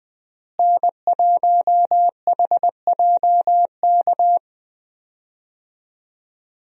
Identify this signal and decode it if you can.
Morse code, heard as a keyed tone, about 20 wpm, 700 Hz. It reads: N1HJK
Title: USA Call Signs - 5 Characters - Repeated 2x - 20wpm